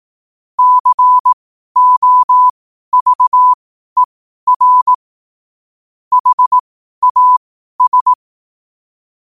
Morse code, heard as a keyed tone, about 18 wpm, spaced as for 12 wpm, 1000 Hz. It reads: COVER HAS
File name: qotdfiles / 12wpm